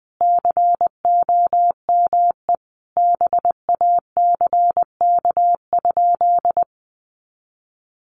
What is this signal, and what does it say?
Morse code, heard as a keyed tone, about 20 wpm, 700 Hz. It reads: COME BACK?
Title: Sentences from Top 200 Words 20wpm